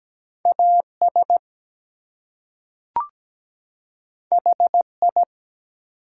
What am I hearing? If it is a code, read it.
Morse code, heard as a keyed tone, about 17 wpm, 700 Hz. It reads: AS HI
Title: Two Letter Words - 17wpm